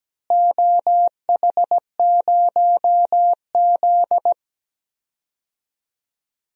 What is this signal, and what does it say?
Morse code, heard as a keyed tone, about 17 wpm, 700 Hz. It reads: OH0Z
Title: DXCC-100 Call Signs - 4 Characters - Repeated 2x - 17wpm